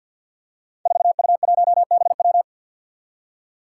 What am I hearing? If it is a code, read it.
Morse code, heard as a keyed tone, about 51 wpm, 700 Hz. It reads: 4U1BW
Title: Five Challenging Letter-Number Combinations 50wpm